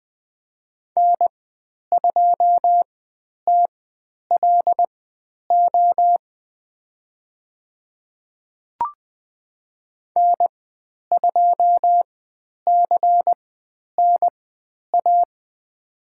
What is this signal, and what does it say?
Morse code, heard as a keyed tone, about 20 wpm, spaced as for 10 wpm, 700 Hz. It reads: N2TLO N2CNA
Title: 109 Lesson 18g - Letter m - Call Signs 20wpm